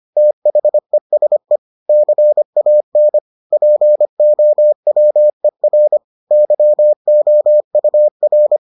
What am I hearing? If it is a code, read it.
Morse code, heard as a keyed tone, about 25 wpm, 600 Hz. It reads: THESE CAN POWER YOUR